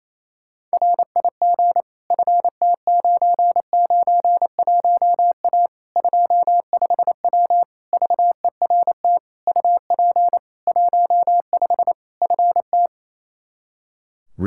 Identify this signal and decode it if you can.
Morse code, heard as a keyed tone, about 28 wpm, 700 Hz. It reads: RIG FT991A 25W VERT UP 15 FT
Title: Practice QSO - Exchange 2 - Info - 28wpm